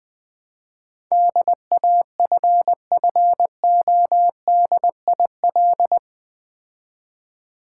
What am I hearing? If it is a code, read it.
Morse code, heard as a keyed tone, about 20 wpm, 700 Hz. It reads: DAFFODIL
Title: Difficult Words with Hints - 20wpm